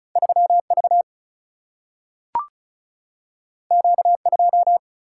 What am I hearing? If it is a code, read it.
Morse code, heard as a keyed tone, about 36 wpm, 700 Hz. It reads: 3 V Q 2
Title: Single-Letter Single-Number 35wpm